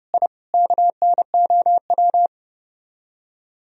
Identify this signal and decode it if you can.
Morse code, heard as a keyed tone, about 34 wpm, 700 Hz. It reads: I KNOW